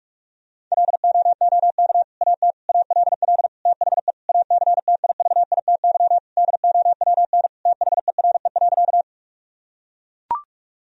Morse code, heard as a keyed tone, about 41 wpm, 700 Hz. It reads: LOOK AT ALL THE ACTIVITY DOWN THERE.
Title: Sentences from Top 1000 Words - Review Part 4 - 45wpm